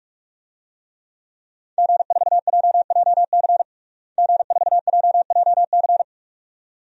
Morse code, heard as a keyed tone, about 46 wpm, 700 Hz. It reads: G4JJC G4JJC
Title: DXCC-100 Call Signs - 5 Characters - Repeated 2x - 45wpm